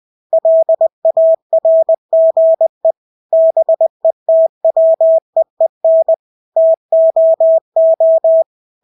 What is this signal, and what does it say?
Morse code, heard as a keyed tone, about 20 wpm, 650 Hz. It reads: LARGE BETWEEN TOO
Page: Engelske ord 20 wpm